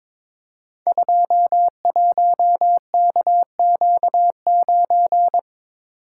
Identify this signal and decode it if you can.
Morse code, heard as a keyed tone, about 22 wpm, 700 Hz. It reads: 21KQ9
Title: Code Groups - 5 Letter:Numbers - 22wpm